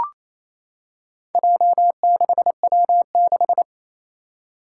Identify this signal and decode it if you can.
Morse code, heard as a keyed tone, about 27 wpm, 700 Hz. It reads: J6W6
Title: Four Challenging Letter-Number Combinations 28wpm